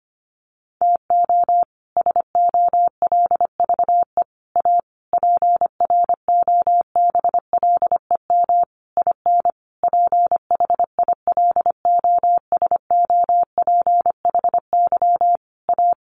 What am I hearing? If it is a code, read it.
Morse code, heard as a keyed tone, about 25 wpm, 700 Hz. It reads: TO SOLVE A PROBLEM IN PHILOSOPHY A
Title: Sentences from Top 1400 Words 25wpm